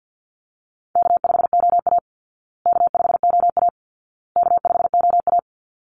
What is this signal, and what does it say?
Morse code, heard as a keyed tone, about 50 wpm, 700 Hz. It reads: K5OA K5OA K5OA
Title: Easy Call Signs Repeated 3x - 50wpm